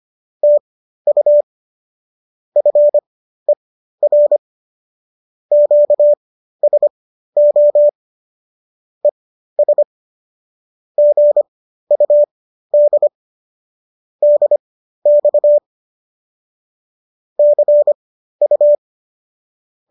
Morse code, heard as a keyed tone, about 25 wpm, spaced as for 13 wpm, 590 Hz. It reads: TU FER QSO ES GUD DX CU